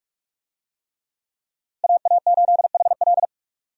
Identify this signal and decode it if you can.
Morse code, heard as a keyed tone, about 46 wpm, 700 Hz. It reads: AA8HL
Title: USA Call Signs - 5 Characters - Repeated 3x - 45wpm